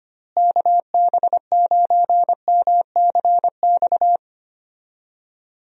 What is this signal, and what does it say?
Morse code, heard as a keyed tone, about 25 wpm, 700 Hz. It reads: KB9MCX
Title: USA Call Signs - 6 Characters - Repeated 3x - 25wpm